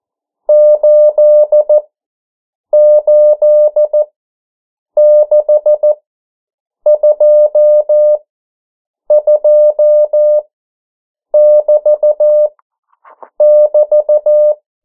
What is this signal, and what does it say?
Morse code, heard as a keyed tone, about 14 wpm, spaced as for 7 wpm, 600 Hz. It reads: 88622<BT><BT>
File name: morse mp3 / 09 urok 8